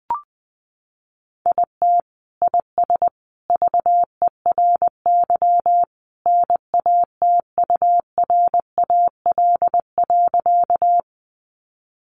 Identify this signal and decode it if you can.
Morse code, heard as a keyed tone, about 20 wpm, 700 Hz. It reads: IT IS VERY NATURAL.
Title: Sentences from Top 700 Words 20wpm